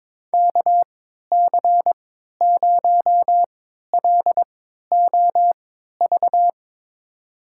Morse code, heard as a keyed tone, about 22 wpm, spaced as for 12 wpm, 700 Hz. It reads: KC0LOV
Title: USA Call Signs - 6 Characters - 3x Character Spacing - 22wpm